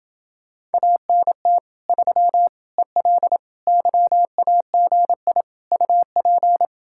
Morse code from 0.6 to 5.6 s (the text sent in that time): ANT 3 EL YAGI